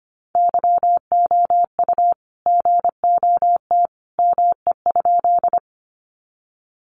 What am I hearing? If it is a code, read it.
Morse code, heard as a keyed tone, about 25 wpm, 700 Hz. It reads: YOU GOT ME?
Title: Sentences from Top 300 Words 25wpm